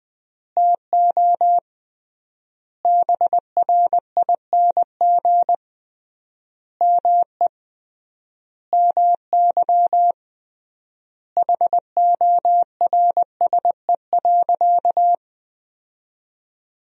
Morse code, heard as a keyed tone, about 20 wpm, 700 Hz. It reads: TO BRING ME MY HORSE.
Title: 3x Word-Spacing - Sentences from Top 500 Words - 20wpm